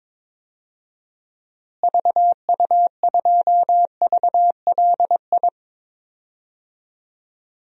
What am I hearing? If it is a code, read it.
Morse code, heard as a keyed tone, about 21 wpm, 700 Hz. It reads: VU2VLI
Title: DXCC-100 Call Signs - 6 Characters - Repeated 2x - 22wpm